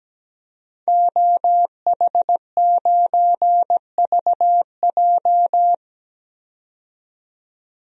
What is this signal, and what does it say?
Morse code, heard as a keyed tone, about 17 wpm, 700 Hz. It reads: OH9VJ
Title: DXCC-100 Call Signs - 5 Characters - 17wpm